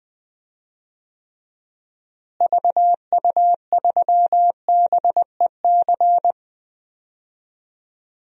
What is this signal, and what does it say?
Morse code, heard as a keyed tone, about 20 wpm, 700 Hz. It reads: VU3BEC
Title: DXCC-100 Call Signs - 6 Characters - Repeated 2x - 20wpm